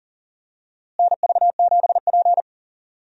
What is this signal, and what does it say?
Morse code, heard as a keyed tone, about 40 wpm, 700 Hz. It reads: NV7P
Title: USA Call Signs - 4 Characters - Repeated 3x - 40wpm